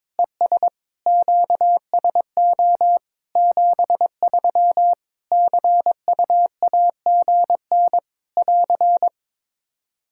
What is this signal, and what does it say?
Morse code, heard as a keyed tone, about 22 wpm, 700 Hz. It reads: ES QSO 73 CUAGN <AR>
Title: K1USN SST QSO Practice 22wpm